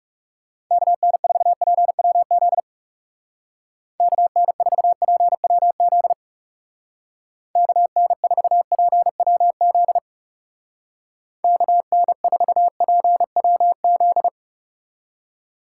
Morse code, heard as a keyed tone, about 37 wpm, 700 Hz. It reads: KN4PWZ KN4PWZ KN4PWZ KN4PWZ
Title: Speed Racing - USA Call Signs 30wpm